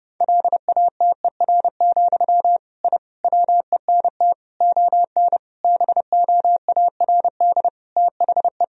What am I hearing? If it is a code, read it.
Morse code, heard as a keyed tone, about 30 wpm, 700 Hz. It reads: LATER, I WENT ON BOARD THE
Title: Sentences from Top 1000 Words - Review Part 4 - 30wpm